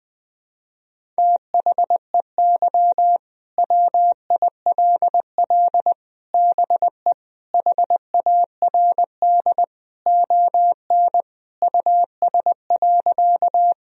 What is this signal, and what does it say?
Morse code, heard as a keyed tone, about 21 wpm, 700 Hz. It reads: THEY WILL BE HARD ON US.
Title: Sentences from Top 1000 Words - Review Part 1 - 20wpm